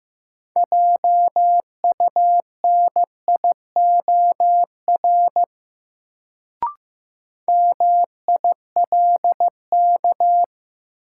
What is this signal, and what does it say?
Morse code, heard as a keyed tone, about 16 wpm, 700 Hz. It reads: JUNIOR MILK
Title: Sentences from Top 1500 Words - New Word Drill Alt - 15wpm